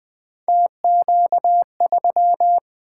Morse code, heard as a keyed tone, about 20 wpm, 700 Hz. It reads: TQ3
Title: Three Letter-Number Combinations - 20wpm